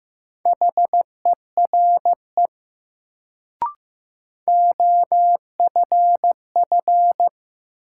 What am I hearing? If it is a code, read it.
Morse code, heard as a keyed tone, about 15 wpm, 700 Hz. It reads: HERE OFF
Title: Sentences from Top 200 Words - New Word Drill - 15wpm